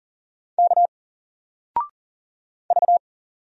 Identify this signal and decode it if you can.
Morse code, heard as a keyed tone, about 41 wpm, 700 Hz. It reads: K V
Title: Single Character (Letter:Number:Punctuation:Prosign) 40wpm